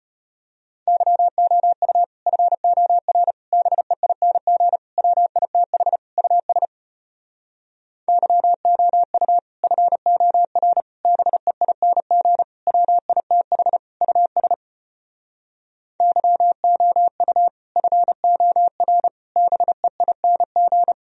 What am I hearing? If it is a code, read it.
Morse code, heard as a keyed tone, about 33 wpm, 700 Hz. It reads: YOU FOR BEING WITH US YOU FOR BEING WITH US YOU FOR BEING